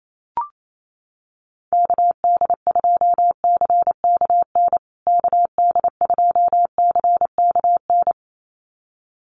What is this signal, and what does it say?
Morse code, heard as a keyed tone, about 28 wpm, 700 Hz. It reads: KD2CKN KD2CKN
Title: USA Call Signs Repeated 2x 28wpm